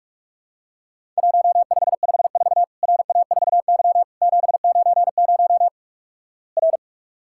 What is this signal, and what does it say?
Morse code, heard as a keyed tone, about 47 wpm, 700 Hz. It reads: 1554 RA4Y 790 R
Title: Worked All Europe DX Contest - QTC Exchange - 45wpm